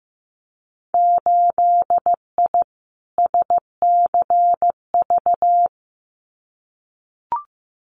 Morse code, heard as a keyed tone, about 15 wpm, 700 Hz. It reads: 8I SCV
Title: Winter Field Day Exchanges Encore 15wpm